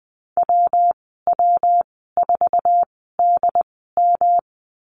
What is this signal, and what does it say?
Morse code, heard as a keyed tone, about 20 wpm, spaced as for 14 wpm, 700 Hz. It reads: WW4DM
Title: 186 Lesson 30g - Letter b - Call Signs 20wpm